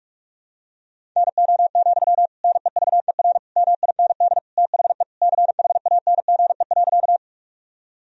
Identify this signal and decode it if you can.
Morse code, heard as a keyed tone, about 46 wpm, 700 Hz. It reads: NO, NEVER MIND THE CHANGE.
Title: Sentences from Top 1000 Words - Review Part 1 - 45wpm